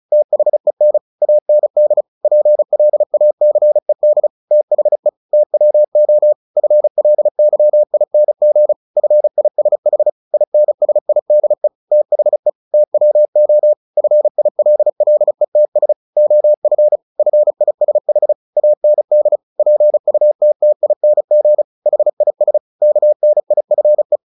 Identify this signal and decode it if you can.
Morse code, heard as a keyed tone, about 34 wpm, 600 Hz. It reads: THEN AND PLACED THE TWO FLYING FISH INSIDE THE TWO FILLETS OF FISH AND PUTTING HIS KNIFE
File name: the old man and the sea 2 35wpm